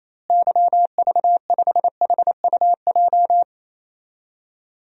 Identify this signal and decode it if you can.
Morse code, heard as a keyed tone, about 28 wpm, 700 Hz. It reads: YV5HUJ
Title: DXCC-100 Call Signs - 6 Characters - Repeated 2x - 28wpm